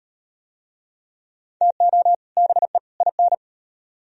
Morse code, heard as a keyed tone, about 38 wpm, 700 Hz. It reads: TO BE IN